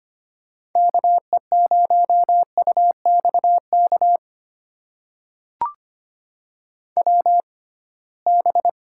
Morse code, heard as a keyed tone, about 25 wpm, spaced as for 19 wpm, 700 Hz. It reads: KE0UXK WB
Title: USA Call Signs - 6 Characters - 6x Character Spacing - 25wpm